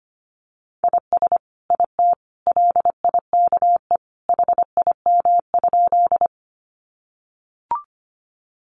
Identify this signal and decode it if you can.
Morse code, heard as a keyed tone, about 25 wpm, 700 Hz. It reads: IS IT LIKE HIM?
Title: Sentences from Top 100 Words 25wpm